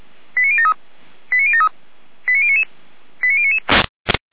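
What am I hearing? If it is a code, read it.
Five-tone selective call (ZVEI), heard as quick runs of five tones, tones of about 70 ms. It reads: 89073 89073 89099 89099
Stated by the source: Received on an SDR.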